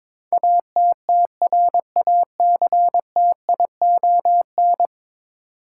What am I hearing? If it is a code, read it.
Morse code, heard as a keyed tone, about 21 wpm, 700 Hz. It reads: ATTRACTION
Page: morse-code-ninja-practice-22wpm